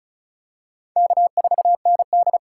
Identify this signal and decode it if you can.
Morse code, heard as a keyed tone, about 34 wpm, 700 Hz. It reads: K4ND
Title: USA Call Signs - 4 Characters - 3x Character Spacing - 35wpm